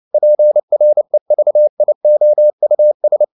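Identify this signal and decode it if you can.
Morse code, heard as a keyed tone, about 29 wpm, 600 Hz. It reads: PREVIOUS
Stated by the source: Morse code at 29 wpm, sending Ken Follett's novel Never in English, from CW med Gnister.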